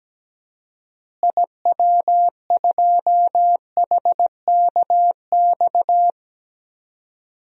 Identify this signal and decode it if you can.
Morse code, heard as a keyed tone, about 17 wpm, 700 Hz. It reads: IW2HKX
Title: DXCC-100 Call Signs - 6 Characters - Repeated 3x - 17wpm